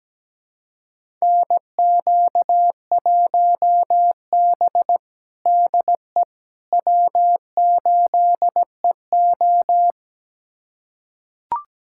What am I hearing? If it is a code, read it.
Morse code, heard as a keyed tone, about 17 wpm, 700 Hz. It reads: NQ1B DE W8EO
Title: USA Callsign DE Callsign - 4 Characters - 17wpm